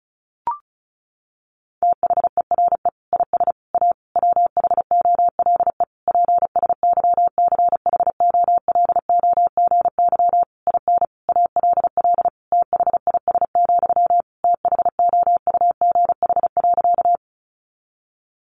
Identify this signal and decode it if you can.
Morse code, heard as a keyed tone, about 34 wpm, 700 Hz. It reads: THERE IS A WHOLE PSYCHOLOGY IN ALL THIS, THOUGH.
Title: Sentences from Top 1400 Words 35wpm